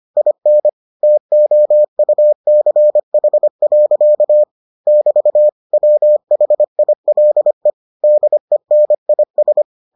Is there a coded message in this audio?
Yes: Morse code from 0.2 to 9.6 s, heard as a keyed tone, about 25 wpm, 600 Hz. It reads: IN TOUCH. <BT> WHILE DENIS